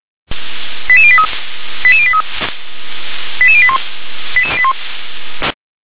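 Five-tone selective call (ZVEI), heard as quick runs of five tones, tones of about 70 ms. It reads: 88083 88083 88081
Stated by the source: ZVEI five-tone selective call received on an SDR.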